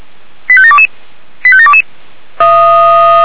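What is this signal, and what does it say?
Five-tone selective call (ZVEI), heard as quick runs of five tones, tones of about 70 ms. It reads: 85722 85722
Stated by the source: ZVEI five-tone selective call 85827, received by SDR.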